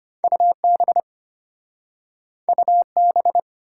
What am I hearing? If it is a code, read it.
Morse code, heard as a keyed tone, about 34 wpm, 700 Hz. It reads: UB UB